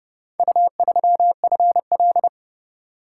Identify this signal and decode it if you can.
Morse code, heard as a keyed tone, about 28 wpm, 700 Hz. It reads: U3FL